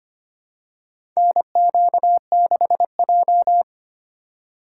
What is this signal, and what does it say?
Morse code, heard as a keyed tone, about 25 wpm, 700 Hz. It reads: NQ6J
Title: USA Call Signs - 4 Characters - 2x Character Spacing - 25wpm